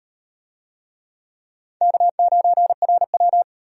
Morse code, heard as a keyed tone, about 32 wpm, 700 Hz. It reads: K9RW